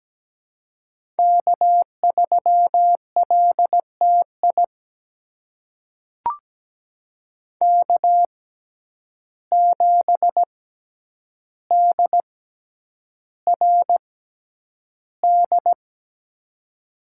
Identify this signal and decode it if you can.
Morse code, heard as a keyed tone, about 17 wpm, spaced as for 9 wpm, 700 Hz. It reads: K3LTI K7DRD
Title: USA Call Signs - 5 Characters - 6x Character Spacing - 17wpm